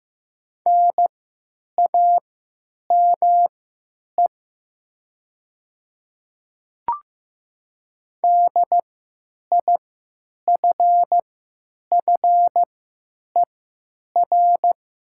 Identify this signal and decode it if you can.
Morse code, heard as a keyed tone, about 15 wpm, spaced as for 9 wpm, 700 Hz. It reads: NAME DIFFER
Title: 3x Character Spacing - Top 300 Words - 15wpm